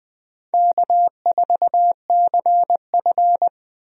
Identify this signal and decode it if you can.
Morse code, heard as a keyed tone, about 20 wpm, 700 Hz. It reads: K4CF
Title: USA Call Signs - 4 Characters - 20wpm